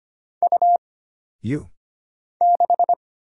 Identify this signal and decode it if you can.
Morse code, heard as a keyed tone, about 25 wpm, 700 Hz. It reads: U 6
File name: Single Character (Letter:Number:Punctuation:Prosign) - Mind-Melt - 25wpm